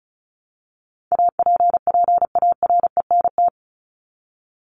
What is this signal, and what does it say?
Morse code, heard as a keyed tone, about 34 wpm, 700 Hz. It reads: APPARENT
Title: LY Suffix - Top 100 Words with Base Words 35wpm